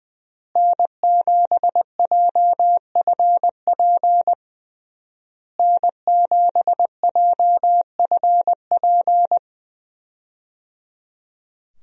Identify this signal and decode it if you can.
Morse code, heard as a keyed tone, about 20 wpm, 700 Hz. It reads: N7JFP N7JFP
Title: USA Call Signs - 5 Characters - Repeated 3x - 20wpm